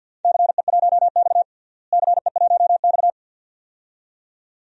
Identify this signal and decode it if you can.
Morse code, heard as a keyed tone, about 53 wpm, 700 Hz. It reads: CE1X CE1X
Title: DXCC-100 Call Signs - 4 Characters - Repeated 2x - 50wpm